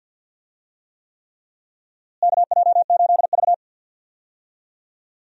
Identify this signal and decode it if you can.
Morse code, heard as a keyed tone, about 53 wpm, 700 Hz. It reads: KJ8V
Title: USA Call Signs - 4 Characters - Repeated 3x - 50wpm